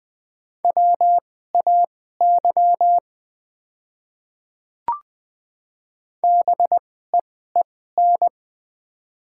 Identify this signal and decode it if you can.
Morse code, heard as a keyed tone, about 20 wpm, spaced as for 14 wpm, 700 Hz. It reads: WAY BEEN
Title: 2x Character Spacing - Top 100 Words - 20wpm